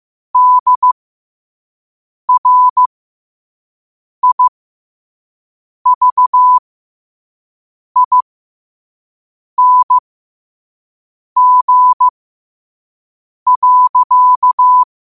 Morse code, heard as a keyed tone, about 15 wpm, spaced as for 5 wpm, 1000 Hz. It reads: DRIVING.
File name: qotdfiles / 5wpm